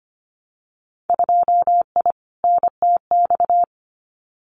Morse code, heard as a keyed tone, about 25 wpm, 700 Hz. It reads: 2I NTX
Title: Winter Field Day Exchanges Encore 25wpm